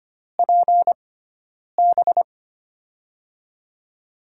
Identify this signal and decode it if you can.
Morse code, heard as a keyed tone, about 25 wpm, spaced as for 9 wpm, 700 Hz. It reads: PB
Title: USA Call Signs - 6x Character Spacing - 25wpm